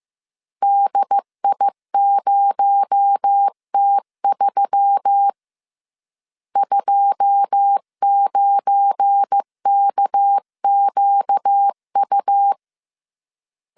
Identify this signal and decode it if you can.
Morse code, heard as a keyed tone, about 15 wpm, 800 Hz. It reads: DI0T3 29KQU
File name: CW / DET / 1 9 1500C S A 14wpm 3222